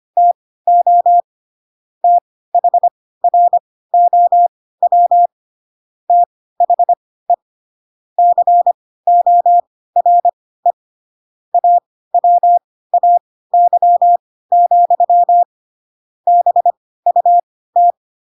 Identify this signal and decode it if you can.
Morse code, heard as a keyed tone, about 25 wpm, spaced as for 16 wpm, 700 Hz. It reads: TO THROW THE CORE AWAY, BUT
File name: Pinocchio / Pinocchio ch7 10 25 ik1hge